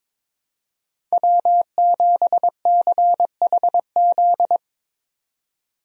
Morse code, heard as a keyed tone, about 22 wpm, 700 Hz. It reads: W7CHZ
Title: USA Call Signs - 5 Characters - 22wpm